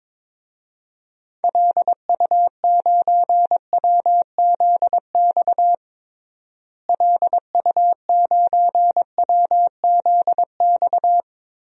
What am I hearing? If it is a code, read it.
Morse code, heard as a keyed tone, about 22 wpm, 700 Hz. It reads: LU9WZX LU9WZX
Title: DXCC-100 Call Signs - 6 Characters - Repeated 3x - 22wpm